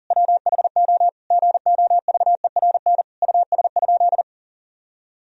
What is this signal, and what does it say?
Morse code, heard as a keyed tone, about 40 wpm, 700 Hz. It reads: WHO GOVERN US?
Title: Sentences from Top 1000 Words - Review Part 2 - 40wpm